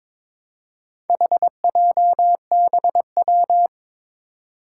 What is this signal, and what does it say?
Morse code, heard as a keyed tone, about 22 wpm, 700 Hz. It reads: HJBW
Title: Four Challenging Letter-Number Combinations 22wpm